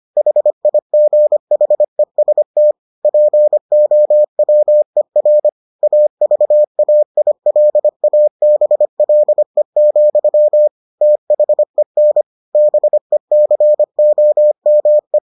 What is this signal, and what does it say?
Morse code, heard as a keyed tone, about 25 wpm, 600 Hz. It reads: HIGHEST POWER AVAILABLE, THEN BECOME